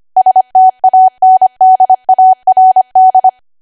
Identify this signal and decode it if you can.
Morse code, heard as a keyed tone, about 25 wpm, 750 Hz. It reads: STANDARD
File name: Archive / 25 WPM / 25WPM